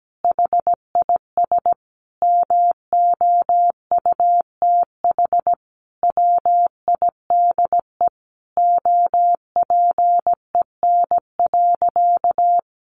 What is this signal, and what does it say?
Morse code, heard as a keyed tone, about 17 wpm, 700 Hz. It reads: HIS MOUTH WIDE OPEN.
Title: Sentences from Top 1500 Words 17wpm